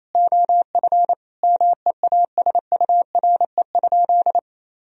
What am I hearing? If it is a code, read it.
Morse code, heard as a keyed tone, about 28 wpm, 700 Hz. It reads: OF MEASURE?
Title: Sentences from Top 1000 Words - Review Part 3 - 28wpm